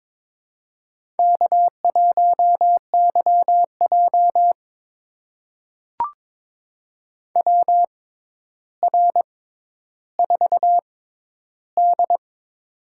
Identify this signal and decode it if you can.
Morse code, heard as a keyed tone, about 23 wpm, spaced as for 11 wpm, 700 Hz. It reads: K1YJ WR4D
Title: USA Call Signs - 4 Characters - 6x Character Spacing - 22wpm